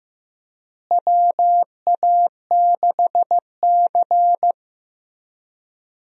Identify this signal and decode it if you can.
Morse code, heard as a keyed tone, about 15 wpm, 700 Hz. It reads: WA6C
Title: USA Call Signs - 4 Characters - Repeated 3x - 15wpm